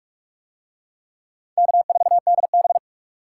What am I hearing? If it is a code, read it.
Morse code, heard as a keyed tone, about 44 wpm, 700 Hz. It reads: K4DB